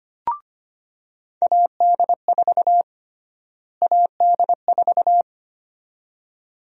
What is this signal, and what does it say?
Morse code, heard as a keyed tone, about 25 wpm, 700 Hz. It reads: AD4 AD4
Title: USA Call Sign Prefixes - Repeated 2x - 25wpm